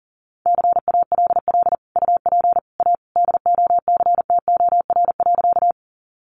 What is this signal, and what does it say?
Morse code, heard as a keyed tone, about 40 wpm, 700 Hz. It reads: CALL UP A DOCTOR.
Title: Sentences from Top 1500 Words 40wpm